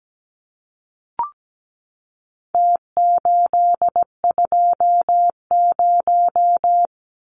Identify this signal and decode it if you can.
Morse code, heard as a keyed tone, about 17 wpm, 700 Hz. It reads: T820
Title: Numbers - Contest Style - 4-Digit 17wpm